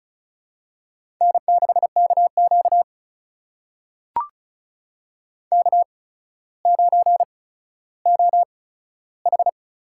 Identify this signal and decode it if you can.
Morse code, heard as a keyed tone, about 35 wpm, spaced as for 15 wpm, 700 Hz. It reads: N6KQ K9OH
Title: USA Call Signs - 4 Characters - 8x Character Spacing - 35wpm